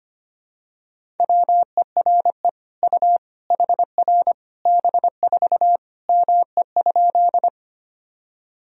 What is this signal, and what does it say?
Morse code, heard as a keyed tone, about 25 wpm, 700 Hz. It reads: WERE U HR B4 ME?
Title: Sentences from Top 200 Words and Abbreviations 25wpm